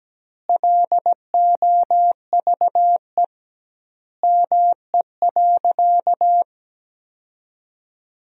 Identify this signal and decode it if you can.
Morse code, heard as a keyed tone, about 17 wpm, 700 Hz. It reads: LOVE ME.
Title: 2x Word-Spacing - Sentences from Top 400 Words - 17wpm